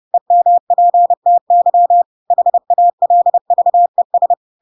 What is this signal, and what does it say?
Morse code, heard as a keyed tone, about 30 wpm, 700 Hz. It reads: EMPTY HALVES
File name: Pinocchio / Pinocchio ch5 30 30 ik1hge